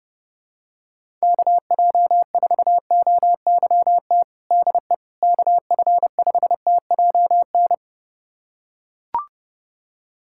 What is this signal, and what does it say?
Morse code, heard as a keyed tone, about 30 wpm, 700 Hz. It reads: KJ4OYT DE KF5TJN
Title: USA Callsign DE Callsign - 6 Characters - 30wpm